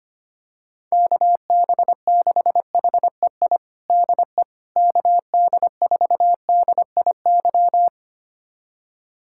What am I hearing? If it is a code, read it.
Morse code, heard as a keyed tone, about 25 wpm, 700 Hz. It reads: KB6HEI DE KD4DIY
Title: USA Callsign DE Callsign - 6 Characters - 25wpm